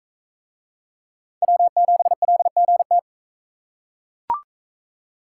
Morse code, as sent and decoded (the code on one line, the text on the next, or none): .-- --... .-.. --. -
W7LGT